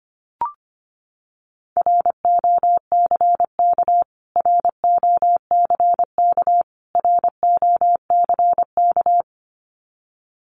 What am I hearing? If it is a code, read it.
Morse code, heard as a keyed tone, about 25 wpm, 700 Hz. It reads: ROCK ROCK ROCK
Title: Top 500 Words Repeated 25wpm